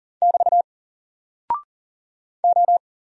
Morse code, heard as a keyed tone, about 39 wpm, 700 Hz. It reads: <BT> O
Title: Single Character (Letter:Number:Punctuation:Prosign) 40wpm